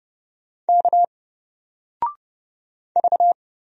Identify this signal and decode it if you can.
Morse code, heard as a keyed tone, about 31 wpm, 700 Hz. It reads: K V
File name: Single-Letter-Number-30wpm